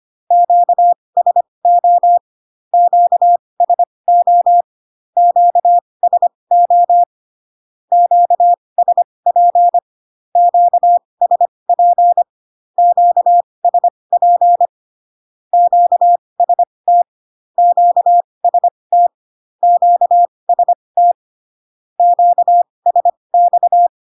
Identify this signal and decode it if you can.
Morse code, heard as a keyed tone, about 25 wpm, spaced as for 20 wpm, 700 Hz. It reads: QSO QSO QSO QSP QSP QSP QST QST QST QSX
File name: cw / Q-codes most common repeated 3 times 15 25wpm ik1hge